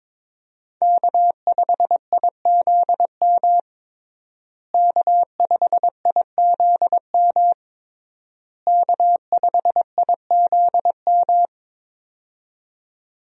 Morse code, heard as a keyed tone, about 21 wpm, 700 Hz. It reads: K5IZM K5IZM K5IZM
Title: USA Call Signs - 5 Characters - Repeated 3x - 22wpm